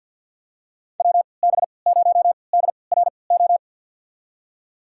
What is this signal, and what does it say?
Morse code, heard as a keyed tone, about 53 wpm, spaced as for 29 wpm, 700 Hz. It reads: WB0DRO
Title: USA Call Signs - 6 Characters - 3x Character Spacing - 50wpm